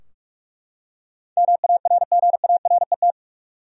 Morse code, heard as a keyed tone, about 42 wpm, 700 Hz. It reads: MARGARET
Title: USA Common Names - 45wpm